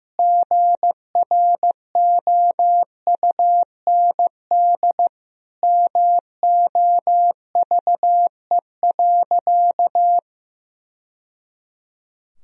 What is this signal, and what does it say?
Morse code, heard as a keyed tone, about 16 wpm, 700 Hz. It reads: GROUND MOVE.
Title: Sentences from Top 1000 Words - Review Part 2 - 15wpm